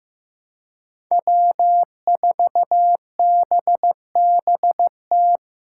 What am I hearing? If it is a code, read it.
Morse code, heard as a keyed tone, about 15 wpm, 700 Hz. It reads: W4BBT
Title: K1USN SST QSO Practice - 15wpm